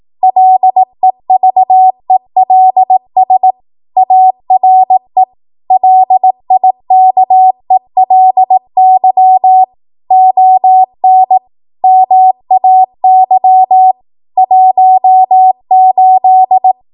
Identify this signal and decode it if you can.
Morse code, heard as a keyed tone, about 18 wpm, 750 Hz. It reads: LEVELS ARE LIKELY ON MAY 18
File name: Morse / 18WPM